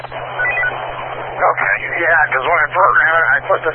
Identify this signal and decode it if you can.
Emergency Alert System (SAME) tones, heard as harsh buzzing bursts, with speech. It is heard from 0.3 to 3.4 s.